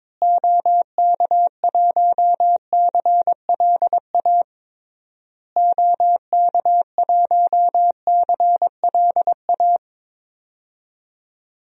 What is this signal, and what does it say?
Morse code, heard as a keyed tone, about 22 wpm, 700 Hz. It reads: OK1CLA OK1CLA
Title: DXCC-100 Call Signs - 6 Characters - Repeated 3x - 22wpm